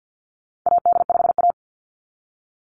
Morse code, heard as a keyed tone, about 50 wpm, 700 Hz. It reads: AD5A
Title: NA SOTA Activator Callsigns 50wpm